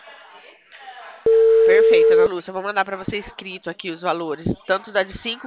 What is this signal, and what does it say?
Ringback tone (China, 450 Hz), with speech.
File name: out- -2015- - - .1409